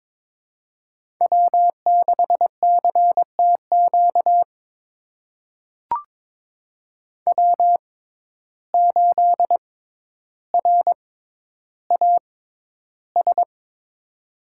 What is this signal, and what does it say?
Morse code, heard as a keyed tone, about 22 wpm, spaced as for 11 wpm, 700 Hz. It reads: W6CTQ W8RAS
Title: USA Call Signs - 5 Characters - 6x Character Spacing - 22wpm